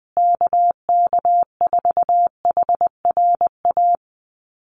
Morse code, heard as a keyed tone, about 20 wpm, 700 Hz. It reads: KK4HRA
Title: USA Call Signs Repeated 2x 20wpm